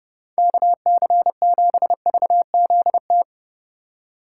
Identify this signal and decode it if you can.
Morse code, heard as a keyed tone, about 30 wpm, 700 Hz. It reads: KC7VZT